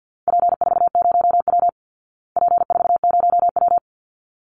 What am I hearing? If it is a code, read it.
Morse code, heard as a keyed tone, about 50 wpm, 700 Hz. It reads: P40W P40W
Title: Call Signs Repeated 3x - 50wpm